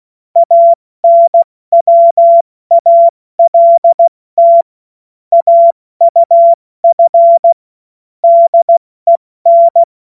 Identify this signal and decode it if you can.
Morse code, heard as a keyed tone, about 16 wpm, 680 Hz. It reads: ANWALT AUF DEN
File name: Morsekurs / Witze / Witz 02. August 2022 70 F80